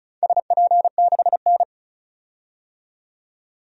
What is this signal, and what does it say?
Morse code, heard as a keyed tone, about 34 wpm, 700 Hz. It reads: SP6N
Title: DXCC-100 Call Signs - 4 Characters - 35wpm